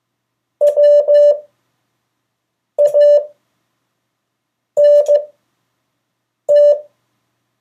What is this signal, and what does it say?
Morse code, heard as a keyed tone, about 15 wpm, spaced as for 5 wpm, 580 Hz. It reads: WANT
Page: CW Class Review – Delaware Valley Radio Association